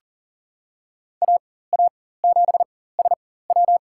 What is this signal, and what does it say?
Morse code, heard as a keyed tone, about 40 wpm, spaced as for 19 wpm, 700 Hz. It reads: AA7SW
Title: USA Call Signs - 5 Characters - 4x Character Spacing - 40wpm